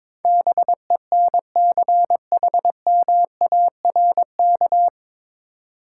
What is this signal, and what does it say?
Morse code, heard as a keyed tone, about 23 wpm, 700 Hz. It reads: BENCHMARK
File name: 9 Letter Words - First 3 Letters Given - 22wpm